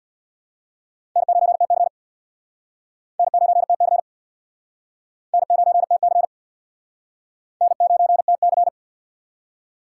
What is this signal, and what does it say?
Morse code, heard as a keyed tone, about 61 wpm, 700 Hz. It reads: N9TC N9TC N9TC N9TC
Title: Speed Racing - USA 1x2 Call Signs 50wpm